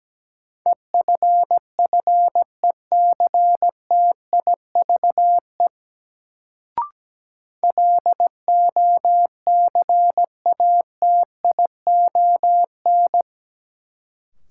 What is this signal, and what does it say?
Morse code, heard as a keyed tone, about 17 wpm, 700 Hz. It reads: EFFECTIVE LOCATION
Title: Sentences from Top 1000 Words - Review Part 6 - New Word Drill - 17wpm